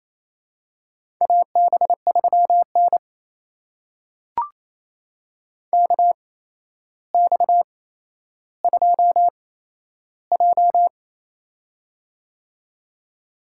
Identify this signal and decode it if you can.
Morse code, heard as a keyed tone, about 28 wpm, spaced as for 12 wpm, 700 Hz. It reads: AB3N KX2J
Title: USA Call Signs - 4 Characters - 8x Character Spacing - 28wpm